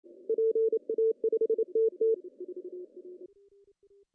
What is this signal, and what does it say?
Morse code, heard as a keyed tone, about 28 wpm, 450 Hz. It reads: PA5TT
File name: CW / cqww-cw-k5tr